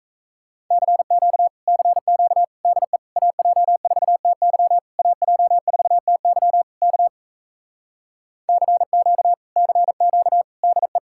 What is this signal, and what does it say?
Morse code, heard as a keyed tone, about 37 wpm, 700 Hz. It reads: CQ CQ DE AJ4TY AJ4TY K CQ CQ DE